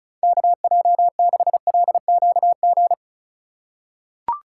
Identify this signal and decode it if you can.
Morse code, heard as a keyed tone, about 36 wpm, 700 Hz. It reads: KJ6LQG
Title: USA Call Signs - 4x Character Spacing - 35wpm